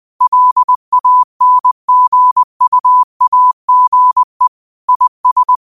Morse code, heard as a keyed tone, about 20 wpm, 1000 Hz. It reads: LANGUAGE IS